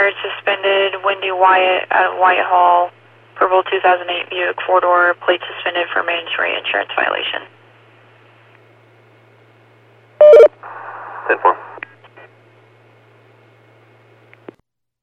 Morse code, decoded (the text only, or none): none